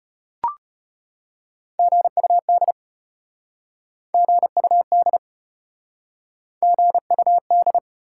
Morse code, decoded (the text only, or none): GUD GUD GUD